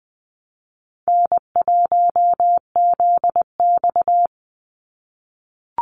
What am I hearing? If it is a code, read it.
Morse code, heard as a keyed tone, about 20 wpm, 700 Hz. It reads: N1ZX
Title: Easy Call Signs - 20wpm